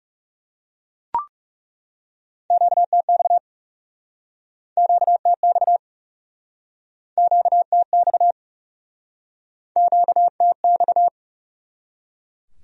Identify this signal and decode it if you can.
Morse code, heard as a keyed tone, about 37 wpm, 700 Hz. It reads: QTX QTX QTX QTX
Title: Speed-Racing Q-Codes - 30wpm